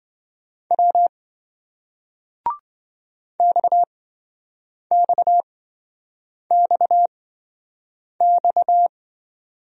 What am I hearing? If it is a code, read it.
Morse code, heard as a keyed tone, about 27 wpm, spaced as for 7 wpm, 700 Hz. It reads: W XXXX